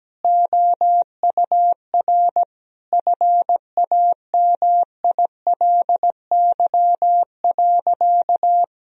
Morse code, heard as a keyed tone, about 17 wpm, 700 Hz. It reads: OUR FAMILY.
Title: Sentences from Top 1000 Words - Review Part 4 - 17wpm